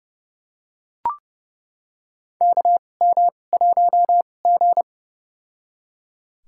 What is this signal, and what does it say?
Morse code, heard as a keyed tone, about 32 wpm, spaced as for 22 wpm, 700 Hz. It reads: KM1G
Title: USA Call Signs - 2x Character Spacing - 30wpm